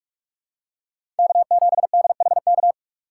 Morse code, heard as a keyed tone, about 46 wpm, 700 Hz. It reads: K7DHK
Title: USA Call Signs - 5 Characters - 3x Character Spacing - 45wpm